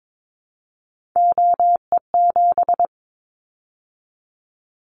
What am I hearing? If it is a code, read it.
Morse code, heard as a keyed tone, about 22 wpm, 700 Hz. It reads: OE7
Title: Callsign Prefixes for Top 25 DXCC - 22wpm